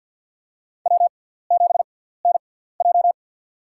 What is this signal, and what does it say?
Morse code, heard as a keyed tone, about 50 wpm, spaced as for 17 wpm, 700 Hz. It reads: W7NJ
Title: USA Call Signs - 6x Character Spacing - 50wpm